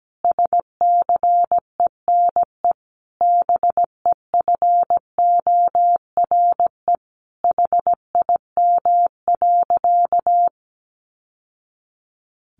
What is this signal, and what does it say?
Morse code, heard as a keyed tone, about 17 wpm, 700 Hz. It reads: SCENE BEFORE HIM.
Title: Sentences from Top 1100 Words 17wpm